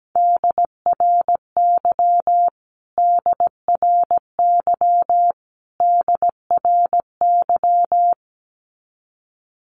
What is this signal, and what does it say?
Morse code, heard as a keyed tone, about 17 wpm, 700 Hz. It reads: DRY DRY DRY
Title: Top 500 Words Repeated 17wpm